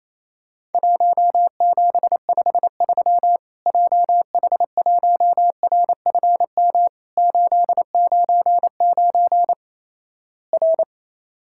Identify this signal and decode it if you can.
Morse code, heard as a keyed tone, about 28 wpm, 700 Hz. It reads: 1753 JH1RFM 899 R
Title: Worked All Europe DX Contest - QTC Exchange - 28wpm